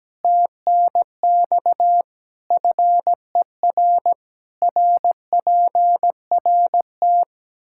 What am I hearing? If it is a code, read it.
Morse code, heard as a keyed tone, about 17 wpm, 700 Hz. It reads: TNX FER RPRT
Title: Practice QSO - Exchange 2 - Info - 17wpm